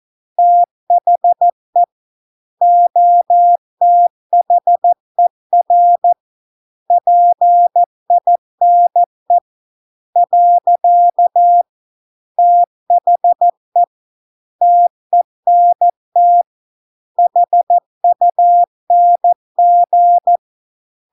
Morse code, heard as a keyed tone, about 14 wpm, 700 Hz. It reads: THE OTHER PINE . THE TENT HUNG